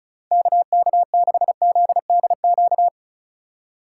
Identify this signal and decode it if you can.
Morse code, heard as a keyed tone, about 34 wpm, 700 Hz. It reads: KK6ZDQ
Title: USA Call Signs - 6 Characters - 3x Character Spacing - 35wpm